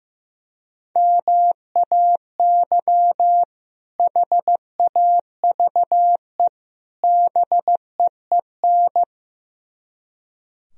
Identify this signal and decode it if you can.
Morse code, heard as a keyed tone, about 15 wpm, 700 Hz. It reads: MAY HAVE BEEN
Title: Speed-Racing - Sets of 3 Words 15wpm